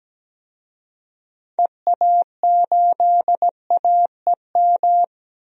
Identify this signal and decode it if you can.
Morse code, heard as a keyed tone, about 17 wpm, 700 Hz. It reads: EA8AEM
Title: DXCC-100 Call Signs - 6 Characters - Repeated 3x - 17wpm